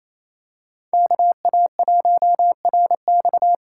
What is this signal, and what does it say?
Morse code, heard as a keyed tone, about 28 wpm, 700 Hz. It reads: KA1RX
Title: USA Call Signs - 6 Characters - 6x Character Spacing - 28wpm